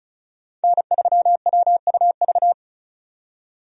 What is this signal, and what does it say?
Morse code, heard as a keyed tone, about 36 wpm, 700 Hz. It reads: N3WUV
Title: USA Call Signs - 5 Characters - 35wpm